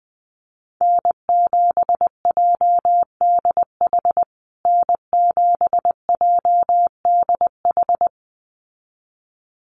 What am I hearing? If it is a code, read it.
Morse code, heard as a keyed tone, about 20 wpm, 700 Hz. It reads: N7JDH N7JDH
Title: USA Call Signs Repeated 2x 20wpm